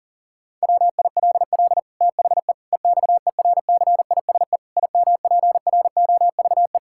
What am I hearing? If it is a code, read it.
Morse code, heard as a keyed tone, about 40 wpm, 700 Hz. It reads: WILL THE EXERCISE IMPROVE
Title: Sentences from Top 1000 Words - Review Part 6 - 40wpm